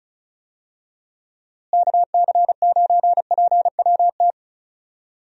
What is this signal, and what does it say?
Morse code, heard as a keyed tone, about 36 wpm, 700 Hz. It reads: KC9PWT
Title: USA Call Signs - 6 Characters - Repeated 2x - 35wpm